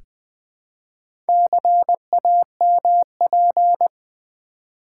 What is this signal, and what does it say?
Morse code, heard as a keyed tone, about 20 wpm, 700 Hz. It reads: CAMP
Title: Build A Word - 1 Letter At A Time - 4-Letter Words - 20wpm